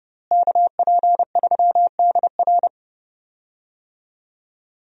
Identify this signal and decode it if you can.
Morse code, heard as a keyed tone, about 32 wpm, 700 Hz. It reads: KP3DR
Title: USA Call Signs - 5 Characters - Repeated 2x - 30wpm